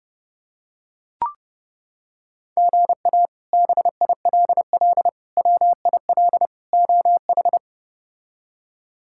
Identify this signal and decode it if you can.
Morse code, heard as a keyed tone, about 30 wpm, 700 Hz. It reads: GA BILL WIL OH
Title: K1USN SST QSO Practice - 30wpm